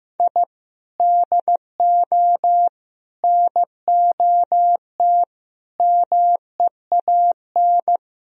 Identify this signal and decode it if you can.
Morse code, heard as a keyed tone, about 17 wpm, 700 Hz. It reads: I DO NOT MEAN
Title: Speed-Racing - Sets of 5 Words 15wpm